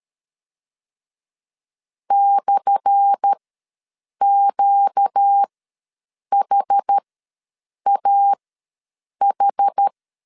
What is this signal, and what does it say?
Morse code, heard as a keyed tone, about 13 wpm, spaced as for 7 wpm, 800 Hz. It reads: /QHAH
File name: LEZIONI / 10 SIGNS / 10D COMA ws 12wpm 3618